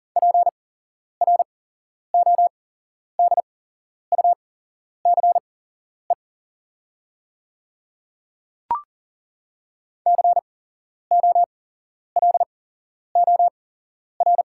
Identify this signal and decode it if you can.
Morse code, heard as a keyed tone, about 39 wpm, spaced as for 11 wpm, 700 Hz. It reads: PRODUCE COLOR
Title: 8x Character Spacing - Top 500 Words - 40wpm